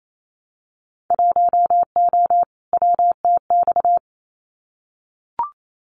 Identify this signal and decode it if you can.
Morse code, heard as a keyed tone, about 28 wpm, 700 Hz. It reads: 1O WTX
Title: Winter Field Day Exchanges Encore 28wpm